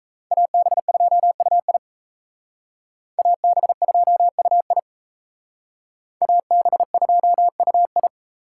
Morse code, heard as a keyed tone, about 35 wpm, 700 Hz. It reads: AB2UI AB2UI AB2UI